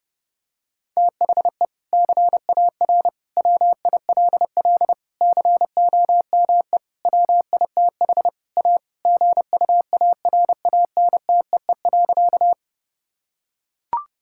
Morse code, heard as a keyed tone, about 30 wpm, 700 Hz. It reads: THE CAR WILL COME WITH A GUARANTEE.
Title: Sentences from Top 1600 Words 30wpm